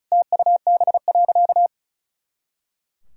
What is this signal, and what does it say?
Morse code, heard as a keyed tone, about 36 wpm, 700 Hz. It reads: TUB.
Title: Pangram Challenge v2 - 35wpm